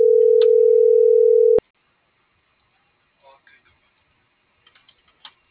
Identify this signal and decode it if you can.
Ringback tone (North America), 440 + 480 Hz.